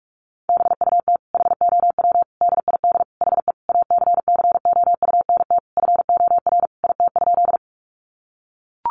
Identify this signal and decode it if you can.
Morse code, heard as a keyed tone, about 44 wpm, 700 Hz. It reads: BUT HOW DID HE ACCOUNT FOR IT?
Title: Sentences from Top 900 Words 45wpm